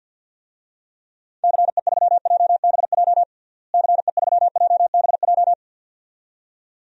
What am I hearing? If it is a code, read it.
Morse code, heard as a keyed tone, about 48 wpm, 700 Hz. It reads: CE3JBJ CE3JBJ
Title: DXCC-100 Call Signs - 6 Characters - Repeated 2x - 50wpm